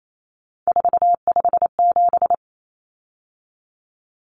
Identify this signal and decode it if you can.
Morse code, heard as a keyed tone, about 28 wpm, 700 Hz. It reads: 457
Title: Common Cut Numbers - 3-Digit 28wpm